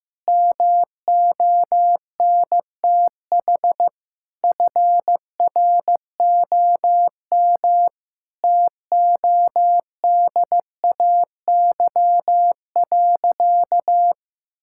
Morse code, heard as a keyed tone, about 15 wpm, 700 Hz. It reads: MONTH FROM TODAY.
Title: Sentences from Top 1000 Words - Review Part 5 - 15wpm